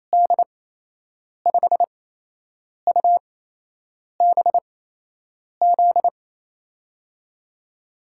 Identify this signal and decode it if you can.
Morse code, heard as a keyed tone, about 28 wpm, spaced as for 8 wpm, 700 Hz. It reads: D5UBZ
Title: USA Call Signs - 6 Characters - 8x Character Spacing - 28wpm